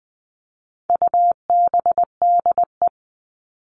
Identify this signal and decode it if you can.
Morse code, heard as a keyed tone, about 20 wpm, 700 Hz. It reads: UBDE
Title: All Cut Numbers - 4-Digit 20wpm